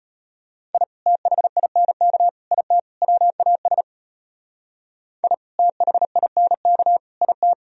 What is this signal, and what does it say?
Morse code, heard as a keyed tone, about 33 wpm, 700 Hz. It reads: I THINK IT WAS I THINK IT